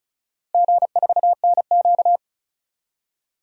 Morse code, heard as a keyed tone, about 36 wpm, 700 Hz. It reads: G4NQ
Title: DXCC-100 Call Signs - 4 Characters - 35wpm